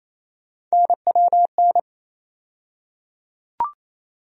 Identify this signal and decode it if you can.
Morse code, heard as a keyed tone, about 28 wpm, 700 Hz. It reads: NWN
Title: USA Call Sign Postfixes - 3 Letters 28wpm